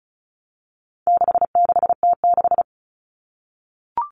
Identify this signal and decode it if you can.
Morse code, heard as a keyed tone, about 34 wpm, 700 Hz. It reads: 66T6
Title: Common Cut Numbers - 4-Digit 35wpm